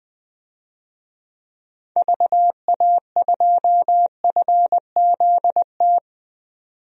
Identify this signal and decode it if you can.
Morse code, heard as a keyed tone, about 20 wpm, 700 Hz. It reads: VA2FZT
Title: DXCC-100 Call Signs - 6 Characters - 20wpm